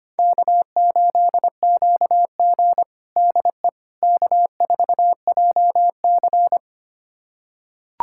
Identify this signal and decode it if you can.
Morse code, heard as a keyed tone, about 25 wpm, 700 Hz. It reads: K8QG DE K4JC
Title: USA Callsign DE Callsign - 4 Characters - 25wpm